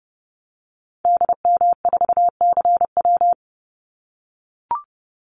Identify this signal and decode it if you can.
Morse code, heard as a keyed tone, about 30 wpm, 700 Hz. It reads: DM4CW
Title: Call Signs - 30wpm